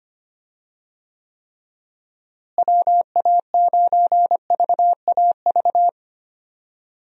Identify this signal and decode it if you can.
Morse code, heard as a keyed tone, about 25 wpm, 700 Hz. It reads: WA9VAV
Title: USA Call Signs - 6 Characters - Repeated 3x - 25wpm